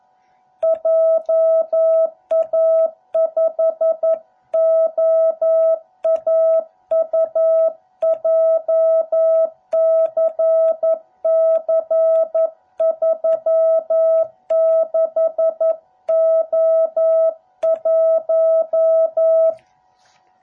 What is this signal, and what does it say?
Morse code, heard as a keyed tone, about 11 wpm, 660 Hz. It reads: JA5OAU JCC36O1